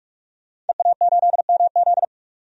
Morse code, heard as a keyed tone, about 46 wpm, 700 Hz. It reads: EA8MZ
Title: DXCC-100 Call Signs - 5 Characters - 45wpm